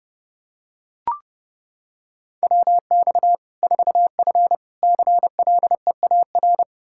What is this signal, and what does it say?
Morse code, heard as a keyed tone, about 29 wpm, 700 Hz. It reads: WX 4F CLEAR
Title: Practice QSO - Exchange 2 - Info - 30wpm